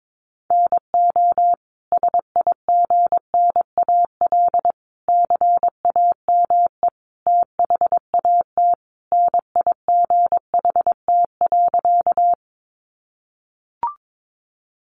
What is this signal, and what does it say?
Morse code, heard as a keyed tone, about 22 wpm, 700 Hz. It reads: NO SIGNAL CAME THAT NIGHT.
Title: Sentences from Top 1400 Words 22wpm